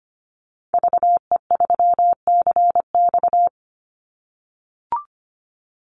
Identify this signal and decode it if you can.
Morse code, heard as a keyed tone, about 25 wpm, 700 Hz. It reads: VE3CX
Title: Call Signs - 25wpm